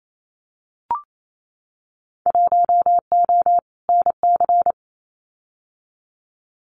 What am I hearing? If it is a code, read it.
Morse code, heard as a keyed tone, about 28 wpm, 700 Hz. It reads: 1O NC
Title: Winter Field Day Exchanges 28wpm